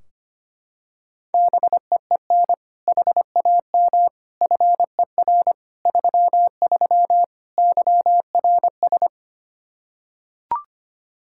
Morse code, heard as a keyed tone, about 25 wpm, 700 Hz. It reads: BEEN HAM FER 33 YRS
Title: Practice QSO - Exchange 3 - More Info - 25wpm